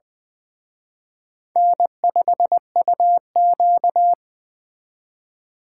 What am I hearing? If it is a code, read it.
Morse code, heard as a keyed tone, about 24 wpm, 700 Hz. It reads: N5UQ
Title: Speed Racing - USA 1x2 Call Signs 20wpm